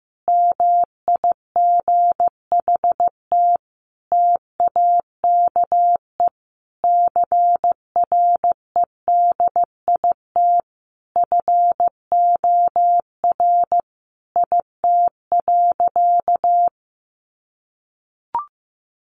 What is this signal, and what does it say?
Morse code, heard as a keyed tone, about 15 wpm, 700 Hz. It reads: MIGHT TAKE CREDIT FOR IT.
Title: Sentences from Top 1000 Words 15wpm